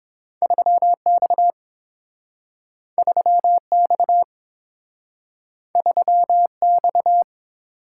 Morse code, heard as a keyed tone, about 32 wpm, 700 Hz. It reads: 3 X 3 X 3 X